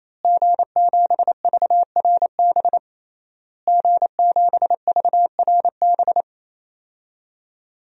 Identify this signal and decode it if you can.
Morse code, heard as a keyed tone, about 28 wpm, 700 Hz. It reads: G7VRB G7VRB
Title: DXCC-100 Call Signs - 5 Characters - Repeated 3x - 28wpm